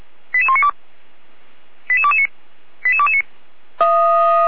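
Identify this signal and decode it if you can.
Five-tone selective call (ZVEI), heard as quick runs of five tones, tones of about 70 ms. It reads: 88192 88229 88229
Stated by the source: Received on an SDR; ZVEI five-tone call 88102, from the Feuerwehr Orlamünde (fire brigade) audio server.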